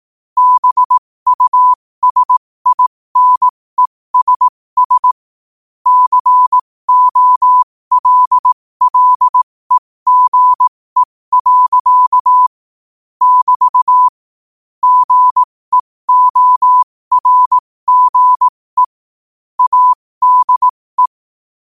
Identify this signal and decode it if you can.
Morse code, heard as a keyed tone, about 18 wpm, spaced as for 15 wpm, 1000 Hz. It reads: BUSINESS COLLEGE. <BT> GEORGE ADE
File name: qotdfiles / 15wpm